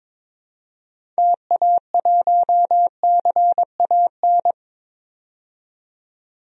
Morse code, heard as a keyed tone, about 22 wpm, 700 Hz. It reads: TA1CAN
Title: DXCC-100 Call Signs - 6 Characters - 22wpm